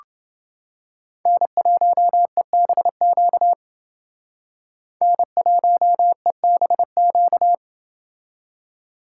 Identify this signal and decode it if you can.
Morse code, heard as a keyed tone, about 28 wpm, 700 Hz. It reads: N1EBQ N1EBQ